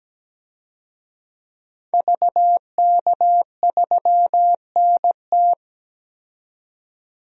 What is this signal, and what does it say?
Morse code, heard as a keyed tone, about 17 wpm, 700 Hz. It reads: VK3NT
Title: DXCC-100 Call Signs - 5 Characters - 17wpm